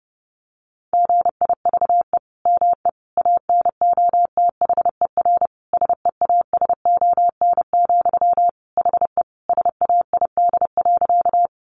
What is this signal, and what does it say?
Morse code, heard as a keyed tone, about 30 wpm, 700 Hz. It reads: GIVE ME ANOTHER SEASON, HE SAID.
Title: Sentences from Top 1400 Words 30wpm